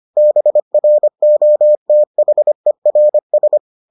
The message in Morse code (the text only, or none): BROTHERS